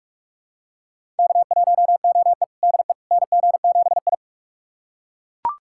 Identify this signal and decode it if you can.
Morse code, heard as a keyed tone, about 41 wpm, 700 Hz. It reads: K1OE DE NG7I
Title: USA Callsign DE Callsign - 4 Characters - 45wpm